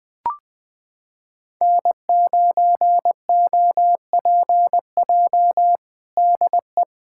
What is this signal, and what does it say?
Morse code, heard as a keyed tone, about 20 wpm, 700 Hz. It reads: N9OPJ DE
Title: USA Callsign DE Callsign - 5 Characters - 20wpm